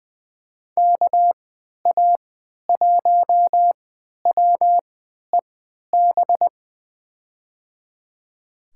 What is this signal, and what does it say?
Morse code, heard as a keyed tone, about 20 wpm, spaced as for 11 wpm, 700 Hz. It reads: KA1WEB
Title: USA Call Signs - 3x Character Spacing - 20wpm